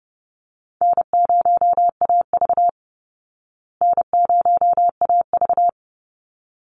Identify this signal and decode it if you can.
Morse code, heard as a keyed tone, about 30 wpm, 700 Hz. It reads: N0AV N0AV
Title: Easy Call Signs Repeated 2x - 30wpm